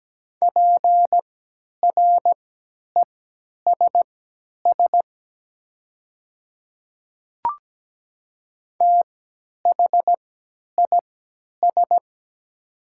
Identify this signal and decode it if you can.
Morse code, heard as a keyed tone, about 17 wpm, spaced as for 10 wpm, 700 Hz. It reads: PRESS THIS
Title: 3x Character Spacing - Top 300 Words - 17wpm